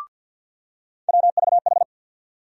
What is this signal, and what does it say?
Morse code, heard as a keyed tone, about 49 wpm, 700 Hz. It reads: WVH
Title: USA Call Sign Postfixes - 3 Letters 50wpm